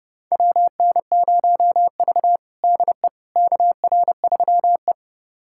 Morse code, heard as a keyed tone, about 32 wpm, 700 Hz. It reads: WN0V DE KR3E
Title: USA Callsign DE Callsign - 4 Characters - 30wpm